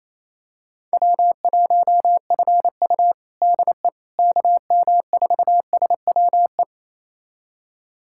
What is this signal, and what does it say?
Morse code, heard as a keyed tone, about 27 wpm, 700 Hz. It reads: W1FU DE KM4SWE
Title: USA Callsign DE Callsign 28wpm